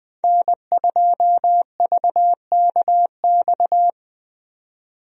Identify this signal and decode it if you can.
Morse code, heard as a keyed tone, about 21 wpm, 700 Hz. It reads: N2VKX
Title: USA Call Signs - 5 Characters - Repeated 2x - 20wpm